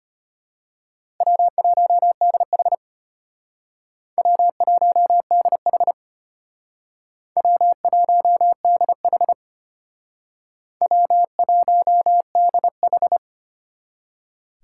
Morse code, heard as a keyed tone, about 33 wpm, 700 Hz. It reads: W1DH W1DH W1DH W1DH